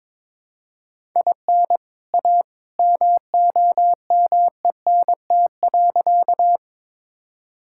Morse code, heard as a keyed tone, about 21 wpm, 700 Hz. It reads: IN A MOMENT.
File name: practice / 22wpm / Sentences with 2-Word Phrases - 22wpm